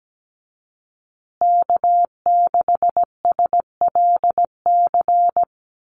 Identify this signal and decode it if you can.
Morse code, heard as a keyed tone, about 17 wpm, 700 Hz. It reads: K6SLC
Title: USA Call Signs Repeated 2x 17wpm